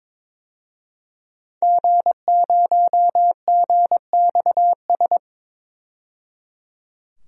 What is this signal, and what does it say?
Morse code, heard as a keyed tone, about 22 wpm, 700 Hz. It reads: G0GXS
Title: DXCC-100 Call Signs - 5 Characters - 22wpm